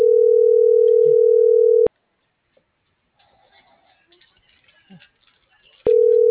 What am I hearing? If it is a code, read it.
Ringback tone (North America), 440 + 480 Hz.